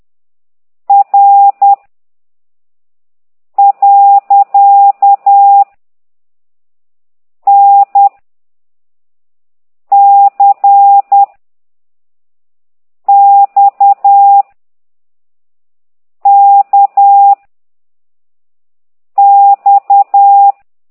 Morse code, heard as a keyed tone, about 10 wpm, 800 Hz. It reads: R . N C X K X
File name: ct2idw / exerc9